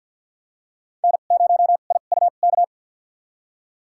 Morse code, heard as a keyed tone, about 52 wpm, spaced as for 36 wpm, 700 Hz. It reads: N0IUK
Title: USA Call Signs - 5 Characters - 2x Character Spacing - 50wpm